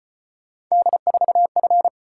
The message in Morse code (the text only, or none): D4F